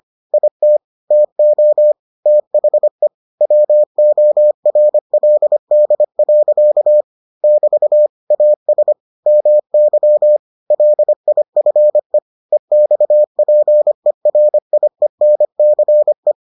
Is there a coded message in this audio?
Yes: Morse code from 0.3 to 16.3 s, heard as a keyed tone, about 25 wpm, 600 Hz. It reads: IT TO THE WORLD. <BT> AS MY LIFE EXPERIENCE